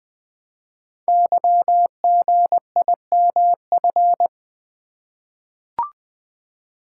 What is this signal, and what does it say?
Morse code, heard as a keyed tone, about 20 wpm, 700 Hz. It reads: YGIMF
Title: Code Groups - 5 Letter:Numbers - 20wpm